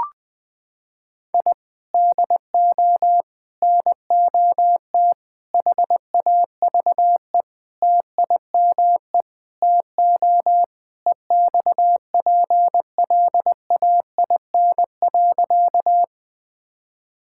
Morse code, heard as a keyed tone, about 20 wpm, 700 Hz. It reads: I DO NOT HAVE TIME TO EXPLAIN.
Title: Sentences with 5-Word Phrases - 20wpm